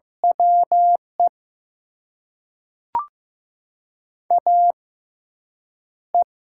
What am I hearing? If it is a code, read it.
Morse code, heard as a keyed tone, about 15 wpm, spaced as for 8 wpm, 700 Hz. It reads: WE AE
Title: USA Call Signs - 4 Characters - 6x Character Spacing - 15wpm